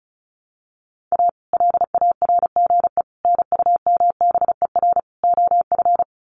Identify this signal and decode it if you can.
Morse code, heard as a keyed tone, about 34 wpm, 700 Hz. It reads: A LARGE NUMBER OF
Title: Sets of 4 Words 35wpm